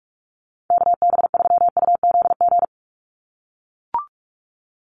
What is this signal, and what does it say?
Morse code, heard as a keyed tone, about 44 wpm, 700 Hz. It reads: KB3UZG
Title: USA Call Signs 45wpm